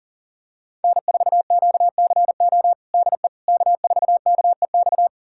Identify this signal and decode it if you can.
Morse code, heard as a keyed tone, about 37 wpm, 700 Hz. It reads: N4QCO DE K4KEX
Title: USA Callsign DE Callsign 40wpm